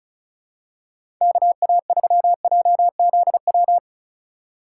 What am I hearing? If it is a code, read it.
Morse code, heard as a keyed tone, about 36 wpm, 700 Hz. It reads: KA3JZW
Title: USA Call Signs - 6 Characters - 2x Character Spacing - 35wpm